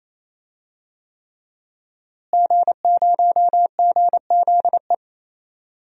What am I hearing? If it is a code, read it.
Morse code, heard as a keyed tone, about 28 wpm, 700 Hz. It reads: G0GZE
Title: DXCC-100 Call Signs - 5 Characters - Repeated 3x - 28wpm